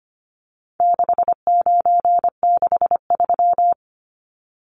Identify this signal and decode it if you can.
Morse code, heard as a keyed tone, about 25 wpm, 700 Hz. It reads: 6963
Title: Numbers - 4-Digit 25wpm